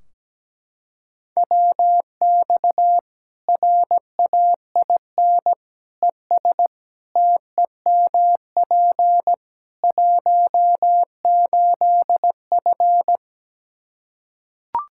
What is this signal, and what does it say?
Morse code, heard as a keyed tone, about 17 wpm, 700 Hz. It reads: WX RAIN ES TEMP 18F
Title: K1USN SST QSO Practice 17wpm